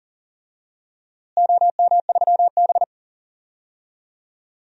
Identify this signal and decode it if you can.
Morse code, heard as a keyed tone, about 40 wpm, 700 Hz. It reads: OM3B
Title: DXCC-100 Call Signs - 4 Characters - Repeated 3x - 40wpm